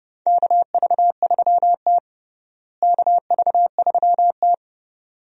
Morse code, heard as a keyed tone, about 30 wpm, 700 Hz. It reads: KV3T KV3T
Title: USA Call Signs - 4 Characters - Repeated 2x - 30wpm